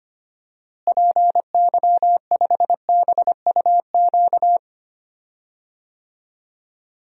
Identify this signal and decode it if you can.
Morse code, heard as a keyed tone, about 25 wpm, 700 Hz. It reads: PY5BUQ
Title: DXCC-100 Call Signs - 6 Characters - Repeated 3x - 25wpm